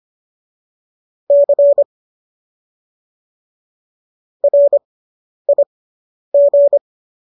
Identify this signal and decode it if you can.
Morse code, heard as a keyed tone, about 25 wpm, spaced as for 10 wpm, 570 Hz. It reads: C RIG